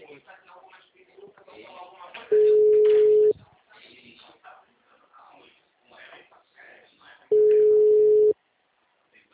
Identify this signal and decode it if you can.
Ringback tone (Europe, 425 Hz), with speech.
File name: out- -2016- - - .21330